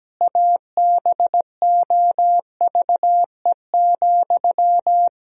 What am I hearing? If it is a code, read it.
Morse code, heard as a keyed tone, about 17 wpm, 700 Hz. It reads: ABOVE,
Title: Sentences from Top 1000 Words - Review Part 2 - 17wpm